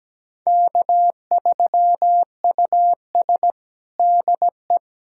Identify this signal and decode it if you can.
Morse code, heard as a keyed tone, about 17 wpm, 700 Hz. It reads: K3US DE
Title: USA Callsign DE Callsign - 4 Characters - 17wpm